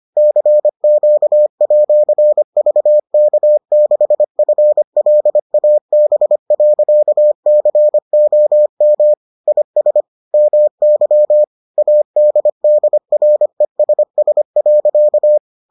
Morse code, heard as a keyed tone, about 25 wpm, 600 Hz. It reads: CQ@VK6FLAB.COM IS MY ADDRESS.